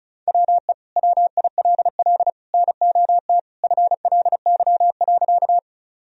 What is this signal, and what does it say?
Morse code, heard as a keyed tone, about 36 wpm, 700 Hz. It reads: WE WILL NOT FLY.
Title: Sentences from Top 1000 Words - Review Part 2 - 35wpm